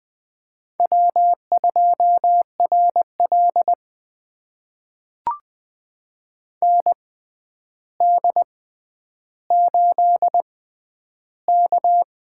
Morse code, heard as a keyed tone, about 20 wpm, spaced as for 10 wpm, 700 Hz. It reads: W2RL ND8K
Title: USA Call Signs - 4 Characters - 6x Character Spacing - 20wpm